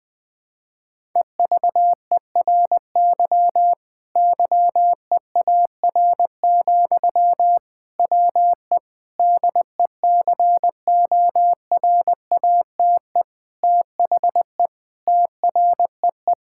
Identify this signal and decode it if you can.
Morse code, heard as a keyed tone, about 21 wpm, 700 Hz. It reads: EVERY YEAR, WE DECORATE THE TREE
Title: Difficult Sentences with Hints - 20wpm